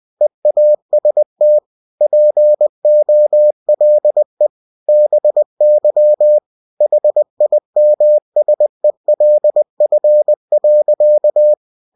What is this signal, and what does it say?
Morse code, heard as a keyed tone, about 20 wpm, 600 Hz. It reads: EAST POLE BY HIMSELF.
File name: cw-books / winnie-the-pooh / winnie-20wpm-page-0093